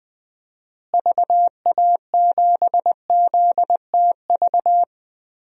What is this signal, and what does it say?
Morse code, heard as a keyed tone, about 20 wpm, 700 Hz. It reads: VA7ZTV
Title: DXCC-100 Call Signs - 6 Characters - Repeated 2x - 20wpm